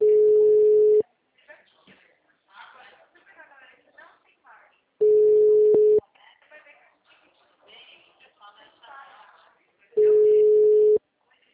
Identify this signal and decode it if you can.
Ringback tone (Europe, 425 Hz), with speech.
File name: out- -2007- - - .17021